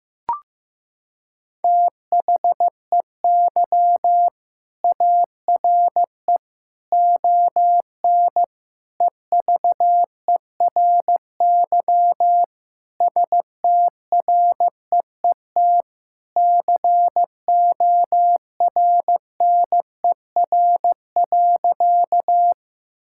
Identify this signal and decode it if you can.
Morse code, heard as a keyed tone, about 15 wpm, 700 Hz. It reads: THEY ARE ON EVERY STREET CORNER.
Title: Sentences from Top 1600 Words 15wpm